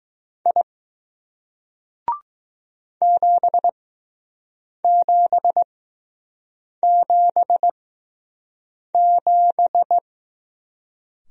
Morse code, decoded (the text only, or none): I 7777